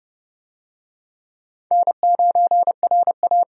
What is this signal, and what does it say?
Morse code, heard as a keyed tone, about 30 wpm, 700 Hz. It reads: N9RA
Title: USA Call Signs - 4 Characters - 30wpm